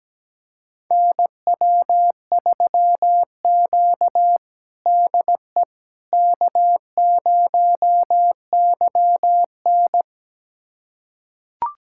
Morse code, heard as a keyed tone, about 17 wpm, 700 Hz. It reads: NW3Q DE K0YN
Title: USA Callsign DE Callsign - 4 Characters - 17wpm